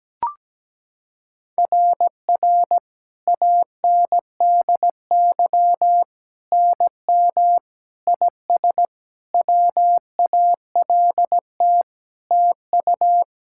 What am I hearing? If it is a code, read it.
Morse code, heard as a keyed tone, about 17 wpm, 700 Hz. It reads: RR ANDY NM IS WALT TU
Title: SOTA QSO Practice 17wpm